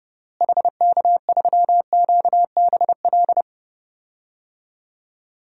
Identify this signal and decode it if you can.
Morse code, heard as a keyed tone, about 32 wpm, 700 Hz. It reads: HK3QBL
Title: DXCC-100 Call Signs - 6 Characters - Repeated 2x - 30wpm